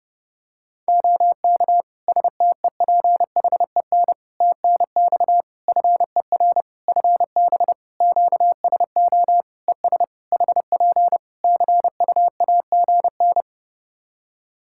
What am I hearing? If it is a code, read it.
Morse code, heard as a keyed tone, about 32 wpm, 700 Hz. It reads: OK STEPHEN TNX FER FB QSO ES HP CUAGN
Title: Practice QSO - Exchange 4 - Ending - 30wpm